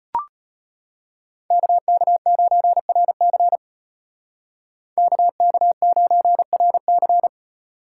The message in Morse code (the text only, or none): KK9RC KK9RC